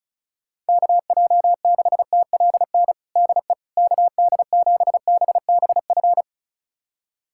Morse code, heard as a keyed tone, about 34 wpm, 700 Hz. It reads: KJ6TLN DE KD7BBF
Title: USA Callsign DE Callsign - 6 Characters - 35wpm